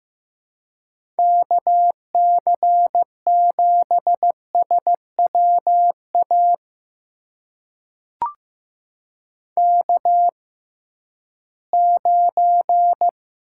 Morse code, heard as a keyed tone, about 15 wpm, spaced as for 11 wpm, 700 Hz. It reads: KC7SWA K9
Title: USA Call Signs - 6x Character Spacing - 15wpm